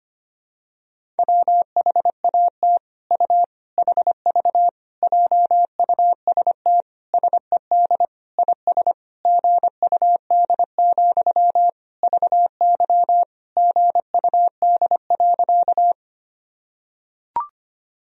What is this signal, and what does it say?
Morse code, heard as a keyed tone, about 25 wpm, 700 Hz. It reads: WHAT U HV JUST SED IS GUD, VY GUD.
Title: Sentences from Top 200 Words and Abbreviations 25wpm